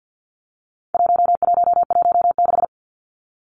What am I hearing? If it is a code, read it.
Morse code, heard as a keyed tone, about 50 wpm, 700 Hz. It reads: 1116
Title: Numbers - 4-Digit 50wpm